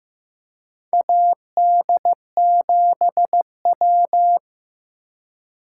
Morse code, heard as a keyed tone, about 15 wpm, 700 Hz. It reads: AD7W
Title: USA Call Signs - 4 Characters - 3x Character Spacing - 15wpm